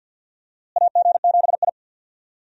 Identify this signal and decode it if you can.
Morse code, heard as a keyed tone, about 49 wpm, 700 Hz. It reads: AG7I
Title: USA Call Signs - 4 Characters - Repeated 3x - 50wpm